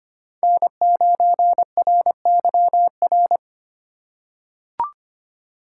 Morse code, heard as a keyed tone, about 25 wpm, 700 Hz. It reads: N9RYR
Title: USA Call Signs - 4x Character Spacing - 25wpm